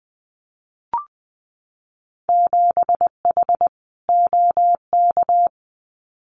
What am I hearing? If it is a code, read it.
Morse code, heard as a keyed tone, about 19 wpm, 700 Hz. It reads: 7H OK
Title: Winter Field Day Exchanges Encore 20wpm